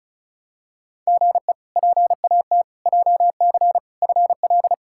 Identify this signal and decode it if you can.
Morse code, heard as a keyed tone, about 36 wpm, 700 Hz. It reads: GE PAT JC FL
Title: K1USN SST QSO Practice - 35wpm